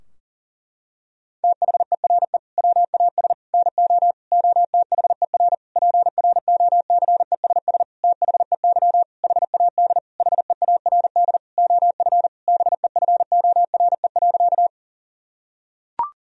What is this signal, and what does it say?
Morse code, heard as a keyed tone, about 40 wpm, 700 Hz. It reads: THERE WAS NO OTHER PROCESS THEY HAD HEARD OF BEFORE.
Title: Sentences from Top 1000 Words - Review Part 3 - 40wpm